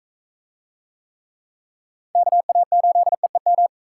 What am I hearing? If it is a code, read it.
Morse code, heard as a keyed tone, about 45 wpm, 700 Hz. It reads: KA8EEM